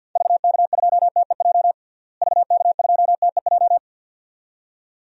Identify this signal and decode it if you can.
Morse code, heard as a keyed tone, about 49 wpm, 700 Hz. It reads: VK2TEJ VK2TEJ
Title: DXCC-100 Call Signs - 6 Characters - Repeated 2x - 50wpm